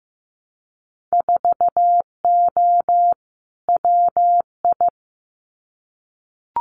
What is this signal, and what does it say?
Morse code, heard as a keyed tone, about 15 wpm, 700 Hz. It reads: 4O WI
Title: Winter Field Day Exchanges 15wpm